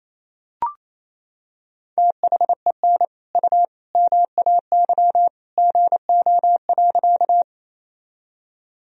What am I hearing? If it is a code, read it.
Morse code, heard as a keyed tone, about 28 wpm, 700 Hz. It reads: THEN U MAY GO.
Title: Sentences from Top 100 Words and Abbreviations 28wpm